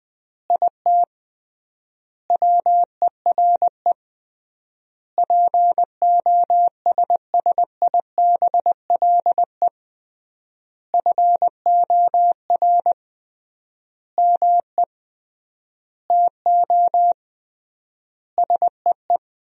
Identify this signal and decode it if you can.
Morse code, heard as a keyed tone, about 20 wpm, 700 Hz. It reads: IT WERE POSSIBLE FOR ME TO SEE
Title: 3x Word-Spacing - Sentences from Top 500 Words - 20wpm